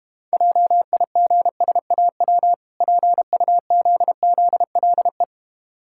Morse code, heard as a keyed tone, about 32 wpm, 700 Hz. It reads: JIGSAW PUZZLE